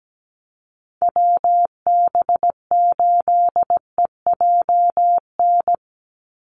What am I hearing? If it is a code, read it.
Morse code, heard as a keyed tone, about 17 wpm, 700 Hz. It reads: WB8EJN
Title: Call Signs - 17wpm